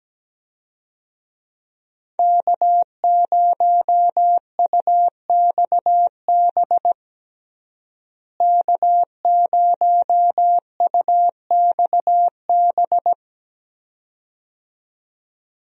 Morse code, heard as a keyed tone, about 17 wpm, 700 Hz. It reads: K0UXB K0UXB
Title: USA Call Signs - 5 Characters - Repeated 2x - 17wpm